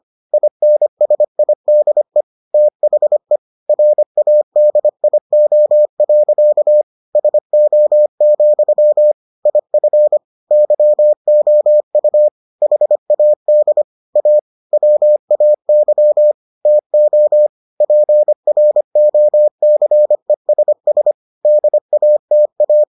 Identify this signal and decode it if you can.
Morse code, heard as a keyed tone, about 25 wpm, 600 Hz. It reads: INSIDE THE RADIO. SO, IF YOU HAD A WAY TO PROCESS DATA